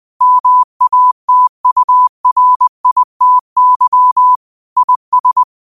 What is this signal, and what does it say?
Morse code, heard as a keyed tone, about 20 wpm, 1000 Hz. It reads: MATURITY IS